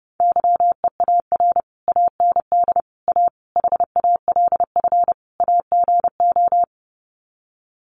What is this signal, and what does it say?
Morse code, heard as a keyed tone, about 29 wpm, 700 Hz. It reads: YEAR AND A HALF AGO
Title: Sets of 5 Words Encore 30wpm